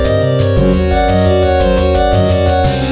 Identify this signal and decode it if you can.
There is no code.